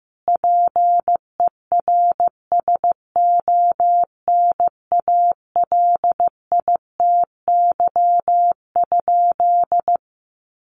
Morse code, heard as a keyed tone, about 15 wpm, 700 Hz. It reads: PERSONALITY?
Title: Sentences from Top 1400 Words 15wpm